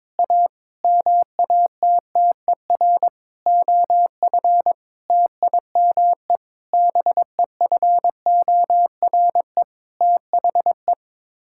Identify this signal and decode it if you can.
Morse code, heard as a keyed tone, about 22 wpm, 700 Hz. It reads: A MATTER OF TIME BEFORE THE
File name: Sets of 9 Words - 22wpm